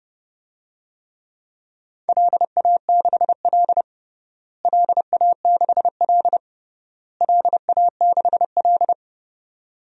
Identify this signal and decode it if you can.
Morse code, heard as a keyed tone, about 30 wpm, 700 Hz. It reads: LA6L LA6L LA6L
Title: DXCC-100 Call Signs - 4 Characters - Repeated 3x - 30wpm